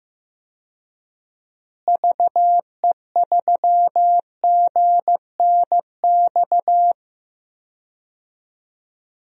Morse code, heard as a keyed tone, about 15 wpm, 700 Hz. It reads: VE3GNX
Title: DXCC-100 Call Signs - 6 Characters - Repeated 2x - 15wpm